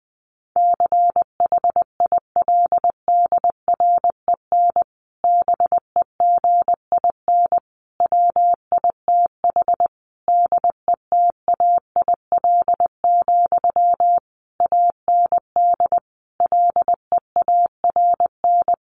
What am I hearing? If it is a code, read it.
Morse code, heard as a keyed tone, about 20 wpm, 700 Hz. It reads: CHILDREN BEGIN WITH DETAIL, AND LEARN
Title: Sentences from Top 1500 Words 20wpm